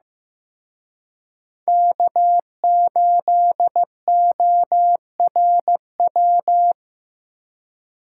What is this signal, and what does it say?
Morse code, heard as a keyed tone, about 18 wpm, 700 Hz. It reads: K8ORW
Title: Speed Racing - USA 1x3 Call Signs 15wpm